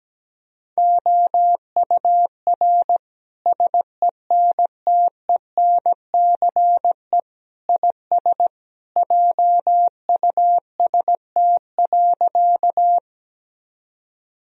Morse code, heard as a keyed tone, about 17 wpm, 700 Hz. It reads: OUR SENTENCE IS JUST.
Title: Sentences from Top 1000 Words - Review Part 1 - 17wpm